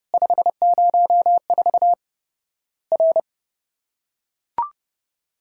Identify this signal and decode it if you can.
Morse code, heard as a keyed tone, about 32 wpm, 700 Hz. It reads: 504 R
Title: Worked All Europe DX Contest - QTC Exchange - 30wpm